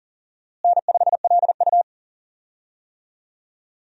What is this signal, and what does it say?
Morse code, heard as a keyed tone, about 40 wpm, 700 Hz. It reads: N5LU
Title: USA Call Signs - 4 Characters - Repeated 3x - 40wpm